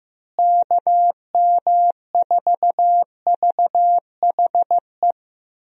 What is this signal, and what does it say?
Morse code, heard as a keyed tone, about 15 wpm, 700 Hz. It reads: KM4VHE
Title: USA Call Signs - 6 Characters - Repeated 2x - 15wpm